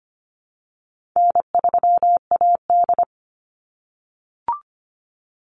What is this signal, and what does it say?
Morse code, heard as a keyed tone, about 25 wpm, 700 Hz. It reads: N3AD
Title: Easy Call Signs - 25wpm